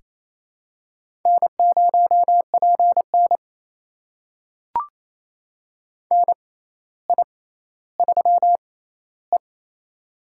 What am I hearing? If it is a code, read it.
Morse code, heard as a keyed tone, about 28 wpm, spaced as for 14 wpm, 700 Hz. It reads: N0PN NI3E
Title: USA Call Signs - 4 Characters - 6x Character Spacing - 28wpm